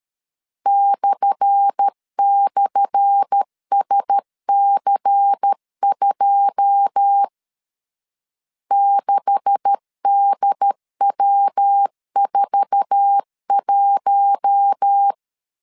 Morse code, heard as a keyed tone, about 13 wpm, 800 Hz. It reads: //SC2 6DW41
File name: LEZIONI / 10 SIGNS / 10A SLASH 12wpm 2314